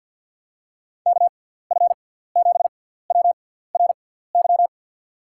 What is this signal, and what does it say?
Morse code, heard as a keyed tone, about 48 wpm, spaced as for 17 wpm, 700 Hz. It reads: KF7WRY
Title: USA Call Signs - 6 Characters - 6x Character Spacing - 50wpm